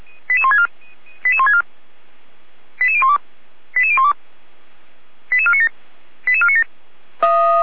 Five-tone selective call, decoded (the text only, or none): 88164 88164 88012 88012 88497 88497